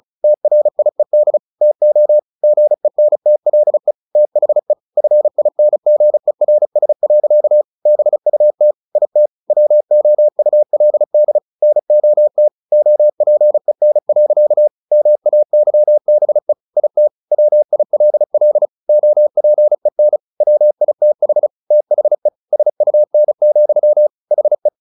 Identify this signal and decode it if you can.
Morse code, heard as a keyed tone, about 34 wpm, 600 Hz. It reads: TRIED TO GENTLE THE FINGERS. BUT IT WOULD NOT OPEN. MAYBE IT WILL OPEN WITH THE SUN, HE
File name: the old man and the sea / the old man and the sea 1 35wpm